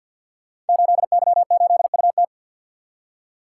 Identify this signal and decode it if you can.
Morse code, heard as a keyed tone, about 52 wpm, 700 Hz. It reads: 8Y8UT
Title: Code Groups - 5 Letter:Numbers - 50wpm